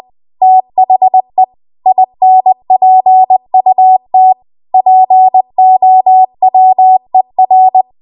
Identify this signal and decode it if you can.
Morse code, heard as a keyed tone, about 20 wpm, 750 Hz. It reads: THE INPUT POWER
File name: Archive / 20 WPM / 20WPM